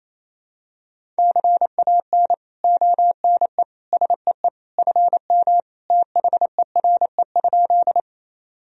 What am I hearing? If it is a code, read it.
Morse code, heard as a keyed tone, about 28 wpm, 700 Hz. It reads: CAN ONE SEE FM THERE?
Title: Sentences from Top 100 Words and Abbreviations 28wpm